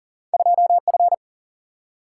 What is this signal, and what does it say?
Morse code, heard as a keyed tone, about 41 wpm, 700 Hz. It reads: 2 F
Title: Single-Letter Single-Number 40wpm